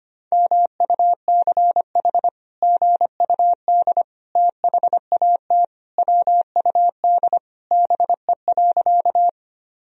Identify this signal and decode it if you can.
Morse code, heard as a keyed tone, about 25 wpm, 700 Hz. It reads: MUCH GUD THAT WUD BE.
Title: Sentences from Top 200 Words and Abbreviations 25wpm